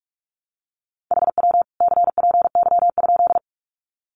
Morse code, heard as a keyed tone, about 44 wpm, 700 Hz. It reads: HW CPY?
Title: Common QSO Phrases 45wpm